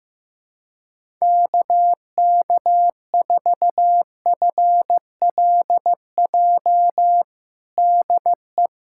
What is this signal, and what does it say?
Morse code, heard as a keyed tone, about 15 wpm, 700 Hz. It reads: KK4FLJ DE
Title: USA Callsign DE Callsign - 6 Characters - 15wpm